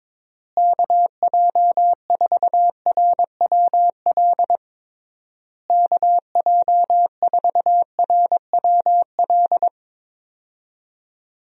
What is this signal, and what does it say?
Morse code, heard as a keyed tone, about 22 wpm, 700 Hz. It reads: KJ4RWL KJ4RWL
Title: USA Call Signs - 6 Characters - Repeated 3x - 22wpm